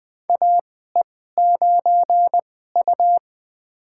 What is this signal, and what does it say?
Morse code, heard as a keyed tone, about 20 wpm, spaced as for 15 wpm, 700 Hz. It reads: AE9U
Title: USA Call Signs - 4 Characters - 2x Character Spacing - 20wpm